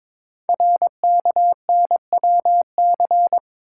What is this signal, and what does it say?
Morse code, heard as a keyed tone, about 22 wpm, 700 Hz. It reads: RKNWC
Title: Code Groups - 5 Letters - 22wpm